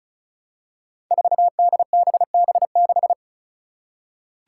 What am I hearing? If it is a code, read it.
Morse code, heard as a keyed tone, about 36 wpm, 700 Hz. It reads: 4DBB6
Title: Five Challenging Letter-Number Combinations 35wpm